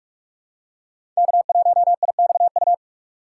Morse code, heard as a keyed tone, about 43 wpm, 700 Hz. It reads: K1IXU
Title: USA Call Signs - 5 Characters - 4x Character Spacing - 45wpm